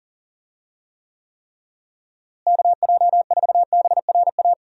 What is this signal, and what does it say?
Morse code, heard as a keyed tone, about 40 wpm, 700 Hz. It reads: KJ4BRA
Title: USA Call Signs - 6 Characters - Repeated 3x - 40wpm